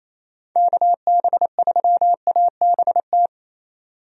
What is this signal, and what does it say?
Morse code, heard as a keyed tone, about 27 wpm, 700 Hz. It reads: KB3ABT
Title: USA Call Signs - 6 Characters - 28wpm